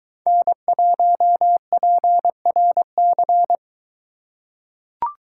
Morse code, heard as a keyed tone, about 22 wpm, 700 Hz. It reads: N1PRC